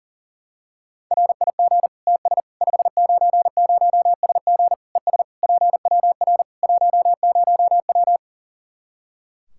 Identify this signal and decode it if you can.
Morse code, heard as a keyed tone, about 40 wpm, 700 Hz. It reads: RIG TS 590SG ES PWR 10W
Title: Practice QSO - Exchange 2 - Info - 40wpm